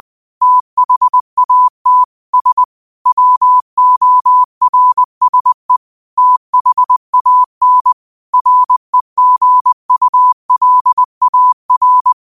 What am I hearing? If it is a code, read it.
Morse code, heard as a keyed tone, about 20 wpm, 1000 Hz. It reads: THAT S WORSE THAN REGULAR
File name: qotdfiles / 20wpm